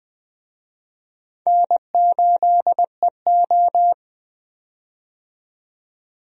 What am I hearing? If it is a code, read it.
Morse code, heard as a keyed tone, about 20 wpm, 700 Hz. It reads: N8EO
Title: USA Call Signs - 4 Characters - 20wpm